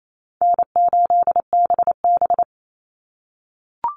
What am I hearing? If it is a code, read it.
Morse code, heard as a keyed tone, about 28 wpm, 700 Hz. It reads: N8BB
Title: NA SOTA Chaser Callsigns 28wpm